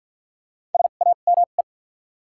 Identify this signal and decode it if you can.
Morse code, heard as a keyed tone, about 48 wpm, spaced as for 35 wpm, 700 Hz. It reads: SAME
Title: 2x Character Spacing - Top 200 Words - 50wpm